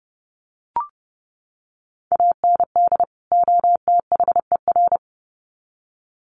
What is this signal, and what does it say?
Morse code, heard as a keyed tone, about 30 wpm, 700 Hz. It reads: AND OTHER
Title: Sets of 2 Words 30wpm